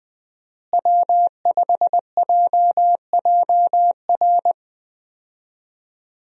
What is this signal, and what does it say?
Morse code, heard as a keyed tone, about 20 wpm, 700 Hz. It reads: W5JJR
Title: USA Call Signs - 5 Characters - Repeated 3x - 20wpm